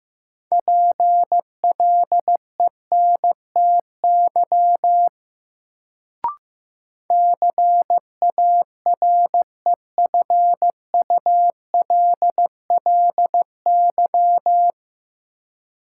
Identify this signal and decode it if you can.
Morse code, heard as a keyed tone, about 15 wpm, 700 Hz. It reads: PLENTY CAREFULLY
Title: Sentences from Top 1300 Words - New Word Drill - 15wpm